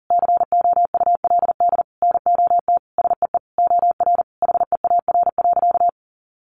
Morse code, heard as a keyed tone, about 37 wpm, 700 Hz. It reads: COULD NOT SEE OR HEAR.
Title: Sentences from Top 600 Words 40wpm